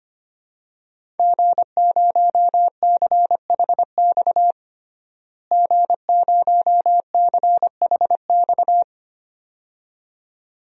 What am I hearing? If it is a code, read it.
Morse code, heard as a keyed tone, about 25 wpm, 700 Hz. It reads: G0CHX G0CHX
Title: DXCC-100 Call Signs - 5 Characters - Repeated 2x - 25wpm